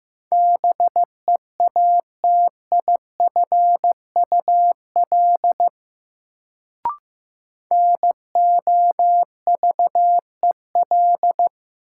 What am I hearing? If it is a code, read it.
Morse code, heard as a keyed tone, about 15 wpm, 700 Hz. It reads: BEATIFUL NOVEL
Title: Sentences from Top 1300 Words - New Word Drill - 15wpm